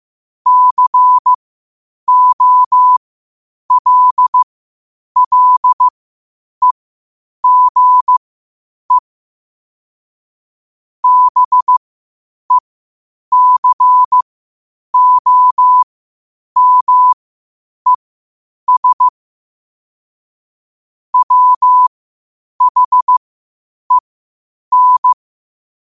Morse code, heard as a keyed tone, about 15 wpm, spaced as for 8 wpm, 1000 Hz. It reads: COLLEGE BECOMES WHEN
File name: qotdfiles / 8wpm